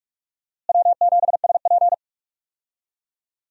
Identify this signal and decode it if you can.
Morse code, heard as a keyed tone, about 41 wpm, 700 Hz. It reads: W7SP
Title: USA Call Signs - 4 Characters - Repeated 3x - 45wpm